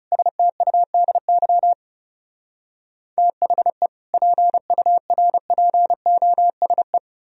Morse code, heard as a keyed tone, about 40 wpm, 700 Hz. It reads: STUDY THE PURPOSE